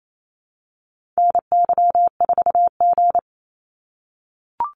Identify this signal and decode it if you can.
Morse code, heard as a keyed tone, about 28 wpm, 700 Hz. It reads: NY4G
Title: NA SOTA Activator Callsigns 28wpm